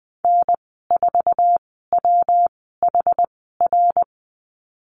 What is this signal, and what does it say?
Morse code, heard as a keyed tone, about 20 wpm, spaced as for 15 wpm, 700 Hz. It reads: N4WHR
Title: 128 Lesson 21g - Number 6 - Call Signs 20wpm